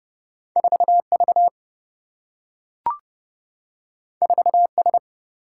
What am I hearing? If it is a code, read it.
Morse code, heard as a keyed tone, about 30 wpm, 700 Hz. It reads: 4V 4S
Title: Single-Letter Single-Number 30wpm